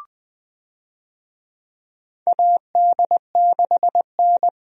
Morse code, heard as a keyed tone, about 20 wpm, 700 Hz. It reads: AD6N
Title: USA Call Signs - 4 Characters - 20wpm